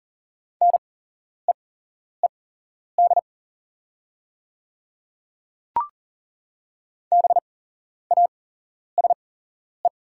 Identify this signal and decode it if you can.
Morse code, heard as a keyed tone, about 38 wpm, spaced as for 11 wpm, 700 Hz. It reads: NEED BASE
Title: 8x Character Spacing - Top 300 Words - 40wpm